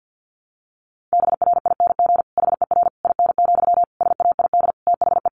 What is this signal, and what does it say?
Morse code, heard as a keyed tone, about 49 wpm, 700 Hz. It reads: BRING HER IN, SAID THE
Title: Sentences from Top 1000 Words 50wpm